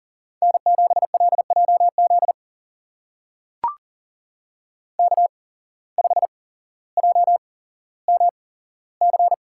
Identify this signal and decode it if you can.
Morse code, heard as a keyed tone, about 38 wpm, spaced as for 17 wpm, 700 Hz. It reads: N7LJZ K5JMC
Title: USA Call Signs - 5 Characters - 8x Character Spacing - 40wpm